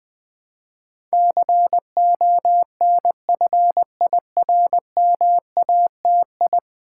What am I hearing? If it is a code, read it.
Morse code, heard as a keyed tone, about 20 wpm, 700 Hz. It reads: CONFIRMATI
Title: Build A Word - 5 Letters At A Time - 12-Letter Words - 20wpm